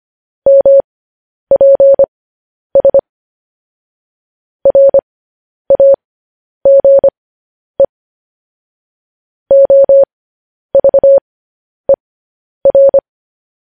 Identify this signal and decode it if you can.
Morse code, heard as a keyed tone, about 25 wpm, spaced as for 10 wpm, 550 Hz. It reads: MPS RAGE OVER
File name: podcast / News headlines-1x-25x10